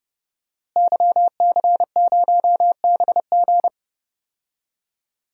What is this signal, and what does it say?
Morse code, heard as a keyed tone, about 30 wpm, 700 Hz. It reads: YC0BG
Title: DXCC-100 Call Signs - 5 Characters - Repeated 2x - 30wpm